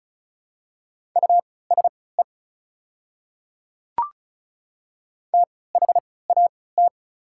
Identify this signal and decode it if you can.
Morse code, heard as a keyed tone, about 34 wpm, spaced as for 20 wpm, 690 Hz. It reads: USE THAT
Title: 3x Character Spacing - Top 100 Words - 35wpm